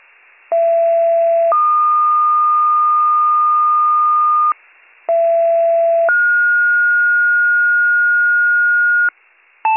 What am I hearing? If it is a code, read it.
Two-tone page, heard as a one-second tone, then a three-second tone, about 670 then 1220 Hz, about 670 then 1430 Hz, with noise. It is heard from 0.5 to 9.1 s.